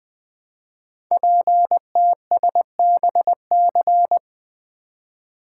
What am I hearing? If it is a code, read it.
Morse code, heard as a keyed tone, about 20 wpm, 700 Hz. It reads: PTSBC
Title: Code Groups - 5 Letters - 20wpm